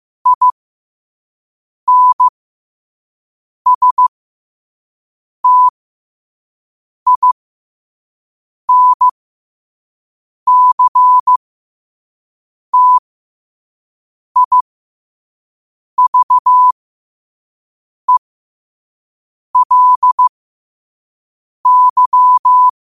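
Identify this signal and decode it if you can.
Morse code, heard as a keyed tone, about 15 wpm, spaced as for 5 wpm, 1000 Hz. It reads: INSTINCTIVELY
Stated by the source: Morse code sent at 5 wpm.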